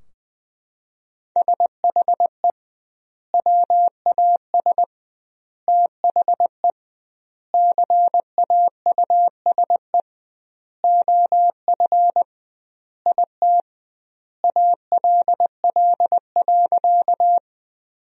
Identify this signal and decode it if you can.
Morse code, heard as a keyed tone, about 20 wpm, 700 Hz. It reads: SHE WAS THE CAUSE OF IT ALL.
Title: 2x Word-Spacing - Sentences from Top 200 Words - 20wpm